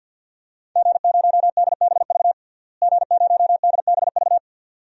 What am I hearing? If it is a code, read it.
Morse code, heard as a keyed tone, about 51 wpm, 700 Hz. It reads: G0DBV G0DBV
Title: DXCC-100 Call Signs - 5 Characters - Repeated 2x - 50wpm